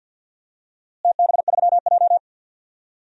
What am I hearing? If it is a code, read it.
Morse code, heard as a keyed tone, about 51 wpm, 700 Hz. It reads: TB3J
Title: DXCC-100 Call Signs - 4 Characters - Repeated 2x - 50wpm